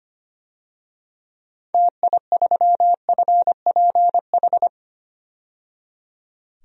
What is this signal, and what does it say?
Morse code, heard as a keyed tone, about 25 wpm, 700 Hz. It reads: TI3FPH
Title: DXCC-100 Call Signs - 6 Characters - 25wpm